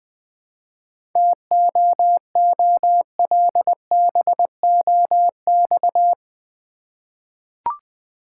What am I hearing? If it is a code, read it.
Morse code, heard as a keyed tone, about 20 wpm, 690 Hz. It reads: TOOLBOX
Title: Easy Words with Hints - 20wpm